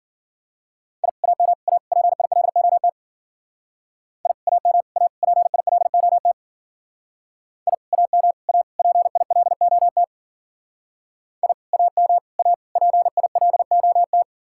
Morse code, heard as a keyed tone, about 50 wpm, 700 Hz. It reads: I AM A PILOT I AM A PILOT I AM A PILOT I AM A PILOT
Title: Speed-Racing Common QSO Phrases - 40wpm